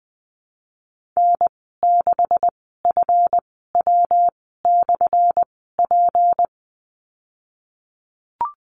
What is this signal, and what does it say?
Morse code, heard as a keyed tone, about 20 wpm, spaced as for 15 wpm, 700 Hz. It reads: N6FW/P
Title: 179 Lesson 29g - Slash - Call Signs 20wpm